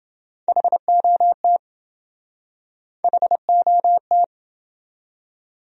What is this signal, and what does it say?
Morse code, heard as a keyed tone, about 28 wpm, 700 Hz. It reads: HOT HOT